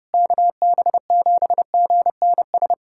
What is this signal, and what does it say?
Morse code, heard as a keyed tone, about 30 wpm, 700 Hz. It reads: KB7GNS
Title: USA Callsign DE Callsign - 6 Characters - 30wpm